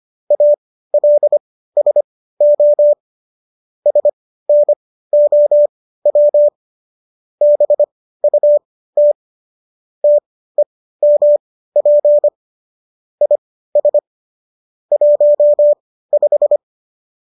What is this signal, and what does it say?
Morse code, heard as a keyed tone, about 25 wpm, spaced as for 15 wpm, 590 Hz. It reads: ALSO SNOW BUT TEMP IS 15